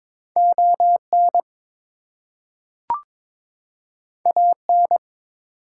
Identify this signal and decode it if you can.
Morse code, heard as a keyed tone, about 22 wpm, 700 Hz. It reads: ON AN
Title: Two Letter Words - 22wpm